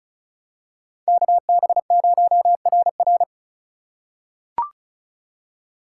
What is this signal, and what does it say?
Morse code, heard as a keyed tone, about 36 wpm, 700 Hz. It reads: KB0RR
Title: USA Call Signs - 5 Characters - 2x Character Spacing - 35wpm